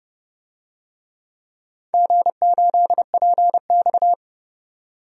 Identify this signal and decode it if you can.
Morse code, heard as a keyed tone, about 32 wpm, 700 Hz. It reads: G8PX
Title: DXCC-100 Call Signs - 4 Characters - 30wpm